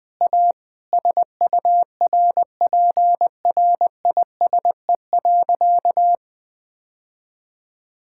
Morse code, heard as a keyed tone, about 20 wpm, 700 Hz. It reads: A SURPRISE.
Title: Sentences with 2-Word Phrases - 20wpm